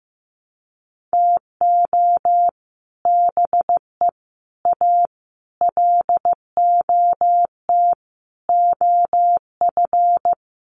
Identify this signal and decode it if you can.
Morse code, heard as a keyed tone, about 14 wpm, 700 Hz. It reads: TO BE A LOT OF
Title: Sets of 5 Words Encore 15wpm